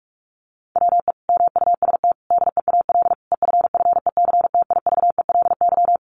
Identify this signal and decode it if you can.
Morse code, heard as a keyed tone, about 44 wpm, 700 Hz. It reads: WE MUST DEAL EFFECTIVELY
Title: Sentences from Top 1400 Words 45wpm